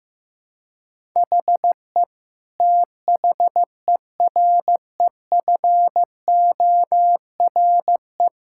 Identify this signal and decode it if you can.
Morse code, heard as a keyed tone, about 15 wpm, 700 Hz. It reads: HE THEREFORE
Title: Sentences from Top 1000 Words - Review Part 5 - 15wpm